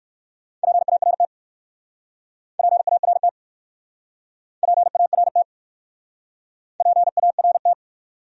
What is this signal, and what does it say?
Morse code, heard as a keyed tone, about 61 wpm, 700 Hz. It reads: PART PART PART PART